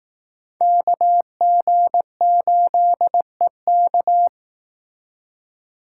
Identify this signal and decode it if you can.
Morse code, heard as a keyed tone, about 19 wpm, 700 Hz. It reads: KG8EK
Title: Speed Racing - USA 2x2 Call Signs 15wpm